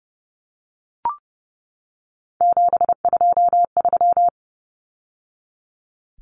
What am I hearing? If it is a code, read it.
Morse code, heard as a keyed tone, about 30 wpm, 700 Hz. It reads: 723
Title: Numbers - 3-Digit 30wpm